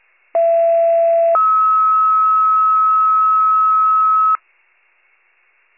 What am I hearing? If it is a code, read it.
Two-tone page, heard as a one-second tone, then a three-second tone, about 670 then 1290 Hz. It is heard from 0.3 to 4.4 s.